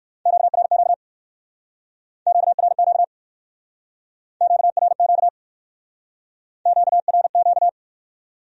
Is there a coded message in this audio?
Yes: Morse code from 0.3 to 7.7 s, heard as a keyed tone, about 58 wpm, 700 Hz. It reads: QRQ QRQ QRQ QRQ